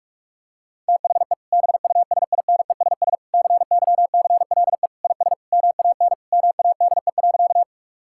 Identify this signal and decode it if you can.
Morse code, heard as a keyed tone, about 46 wpm, 700 Hz. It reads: THE BUSINESS CYCLE IS MAN MADE.
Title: Sentences from Top 1000 Words - Review Part 6 - 45wpm